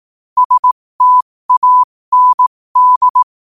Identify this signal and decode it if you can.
Morse code, heard as a keyed tone, about 18 wpm, spaced as for 15 wpm, 1000 Hz. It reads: STAND
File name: qotdfiles / 15wpm